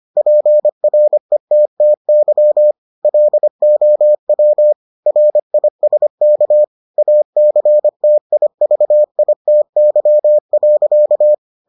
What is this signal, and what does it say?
Morse code, heard as a keyed tone, about 25 wpm, 600 Hz. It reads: PRETTY LOW RISK ACTIVITY.